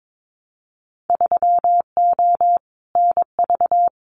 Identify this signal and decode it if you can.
Morse code, heard as a keyed tone, about 22 wpm, 700 Hz. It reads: 3O NV
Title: Winter Field Day Exchanges Encore 22wpm